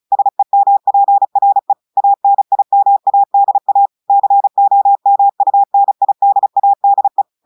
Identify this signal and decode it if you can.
Morse code, heard as a keyed tone, about 36 wpm, 800 Hz. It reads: SEMPRE ANIMADA COMUNIDADE
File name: audios / portugais / portugais texte en clair 1 vitesse 35 mots minutes